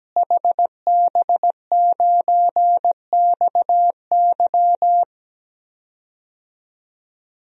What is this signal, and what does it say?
Morse code, heard as a keyed tone, about 17 wpm, 700 Hz. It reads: HB9XY
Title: DXCC-100 Call Signs - 5 Characters - Repeated 3x - 17wpm